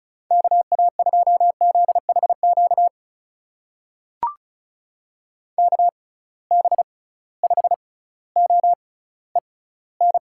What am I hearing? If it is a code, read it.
Morse code, heard as a keyed tone, about 34 wpm, spaced as for 18 wpm, 700 Hz. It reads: KA2ZHQ KB5OEN
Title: USA Call Signs - 6 Characters - 6x Character Spacing - 35wpm